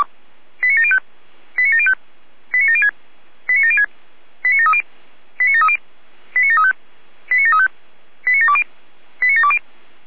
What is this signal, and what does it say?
Five-tone selective call (ZVEI), heard as quick runs of five tones, tones of about 70 ms. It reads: 89705 89705 89706 89706 89730 89730 89735 89735 89720 89720
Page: SDR/ZVEI